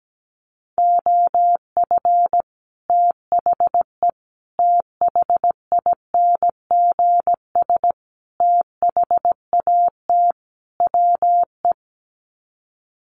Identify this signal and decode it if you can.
Morse code, heard as a keyed tone, about 17 wpm, 700 Hz. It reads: OF THE THINGS THAT WE
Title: Sets of 5 Words Encore 17wpm